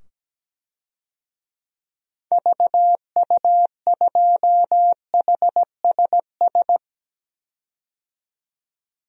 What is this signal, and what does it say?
Morse code, heard as a keyed tone, about 17 wpm, 700 Hz. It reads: VU2HSS
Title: DXCC-100 Call Signs - 6 Characters - 17wpm